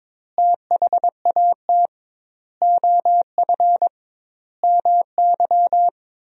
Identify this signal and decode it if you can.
Morse code, heard as a keyed tone, about 22 wpm, 700 Hz. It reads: THAT OF MY
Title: 2x Word-Spacing - Sentences from Top 500 Words - 22wpm